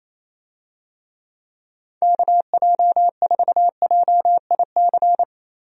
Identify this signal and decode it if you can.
Morse code, heard as a keyed tone, about 28 wpm, 700 Hz. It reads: KJ4JIC
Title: USA Call Signs - 6 Characters - 28wpm